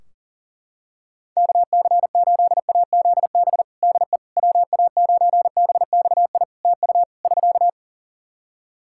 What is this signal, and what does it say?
Morse code, heard as a keyed tone, about 40 wpm, 700 Hz. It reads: KC8AZB DE WA9BXI TU <SK>
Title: K1USN SST QSO Practice 40wpm